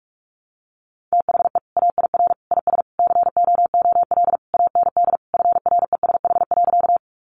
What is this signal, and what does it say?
Morse code, heard as a keyed tone, about 46 wpm, 700 Hz. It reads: THE AIR IS COOL AND FRESH.
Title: Sentences from Top 1200 Words 45wpm